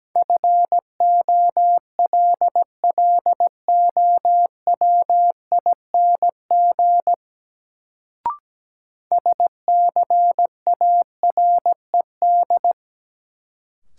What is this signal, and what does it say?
Morse code, heard as a keyed tone, about 17 wpm, 700 Hz. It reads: FOLLOWING SCARED
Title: Sentences from Top 700 Words - New Word Drill - 17wpm